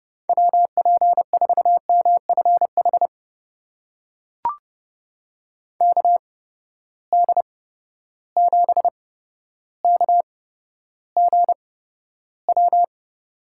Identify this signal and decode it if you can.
Morse code, heard as a keyed tone, about 30 wpm, spaced as for 13 wpm, 700 Hz. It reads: WP4MFH KD7KGW
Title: USA Call Signs - 8x Character Spacing - 30wpm